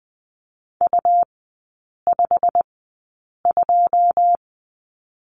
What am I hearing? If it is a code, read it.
Morse code, heard as a keyed tone, about 20 wpm, spaced as for 8 wpm, 690 Hz. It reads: U52
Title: 172 Lesson 28e - Number 9 - Three Characters 20wpm